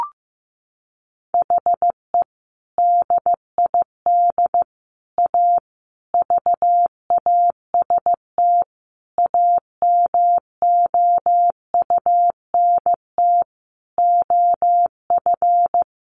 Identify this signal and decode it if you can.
Morse code, heard as a keyed tone, about 15 wpm, 700 Hz. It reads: HE DID A VAST AMOUNT OF
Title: Sentences from Top 1300 Words 15wpm